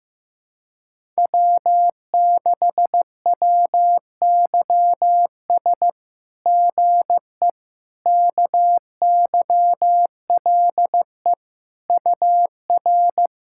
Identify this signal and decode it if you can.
Morse code, heard as a keyed tone, about 16 wpm, 700 Hz. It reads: W6WYS GE KYLE UR
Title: SOTA QSO Practice 15wpm